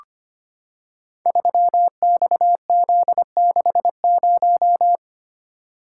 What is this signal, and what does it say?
Morse code, heard as a keyed tone, about 25 wpm, 700 Hz. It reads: 3XZ60
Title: Code Groups - 5 Letter:Numbers - 25wpm